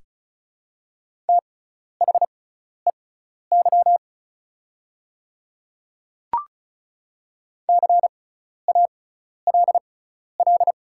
Morse code, heard as a keyed tone, about 36 wpm, spaced as for 12 wpm, 700 Hz. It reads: THEY CALL
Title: 6x Character Spacing - Top 100 Words - 35wpm